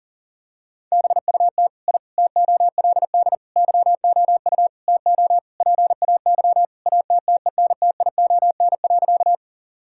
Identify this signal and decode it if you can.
Morse code, heard as a keyed tone, about 40 wpm, 700 Hz. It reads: BUT I TOLD YOU TO PAY ATTENTION.
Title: Sentences from Top 1000 Words - Review Part 6 - 40wpm